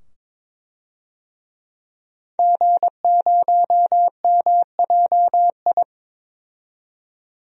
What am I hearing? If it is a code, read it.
Morse code, heard as a keyed tone, about 22 wpm, 700 Hz. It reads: G0MJI
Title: DXCC-100 Call Signs - 5 Characters - 22wpm